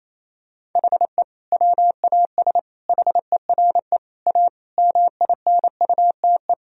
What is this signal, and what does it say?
Morse code, heard as a keyed tone, about 27 wpm, 700 Hz. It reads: HE WAS HERE A MINUTE
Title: Sentences from Top 1000 Words - Review Part 3 - 28wpm